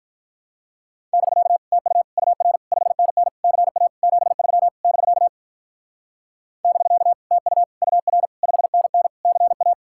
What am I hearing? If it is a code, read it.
Morse code, heard as a keyed tone, about 44 wpm, 700 Hz. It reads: <BK> TU UR 5NN CA 73 <BK> <BK> TU UR 5NN CA
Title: Speed-Racing Common QSO Phrases - 35wpm